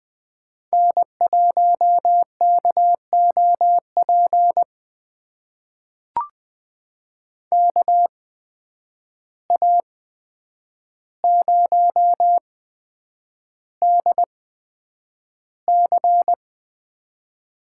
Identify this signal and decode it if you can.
Morse code, heard as a keyed tone, about 20 wpm, spaced as for 9 wpm, 700 Hz. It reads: N1KOP KA0DC
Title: USA Call Signs - 5 Characters - 8x Character Spacing - 20wpm